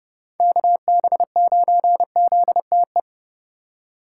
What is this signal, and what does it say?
Morse code, heard as a keyed tone, about 29 wpm, 700 Hz. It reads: KB9ZTE